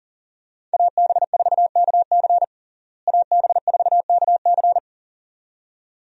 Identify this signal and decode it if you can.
Morse code, heard as a keyed tone, about 40 wpm, 700 Hz. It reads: AB4KC AB4KC
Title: USA Call Signs - 5 Characters - Repeated 2x - 40wpm